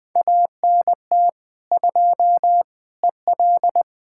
Morse code, heard as a keyed tone, about 20 wpm, 700 Hz. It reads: ANT 2 EL
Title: Speed-Racing Common QSO Phrases Encore - 17wpm